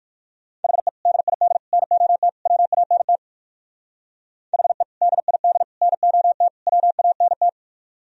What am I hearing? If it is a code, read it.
Morse code, heard as a keyed tone, about 60 wpm, 700 Hz. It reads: HE DID NOT WANT HE DID NOT WANT